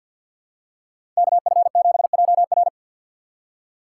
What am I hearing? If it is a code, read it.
Morse code, heard as a keyed tone, about 52 wpm, 690 Hz. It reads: KF7JR
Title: USA Call Signs - 2x Character Spacing - 50wpm